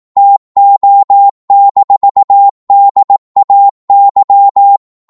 Morse code, heard as a keyed tone, about 18 wpm, 800 Hz. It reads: TO-DAY
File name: Morse / psg-18wpm